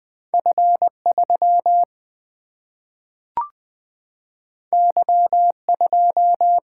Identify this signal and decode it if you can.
Morse code, heard as a keyed tone, about 20 wpm, 700 Hz. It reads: F 3 Y 2
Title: Single-Letter Single-Number 20wpm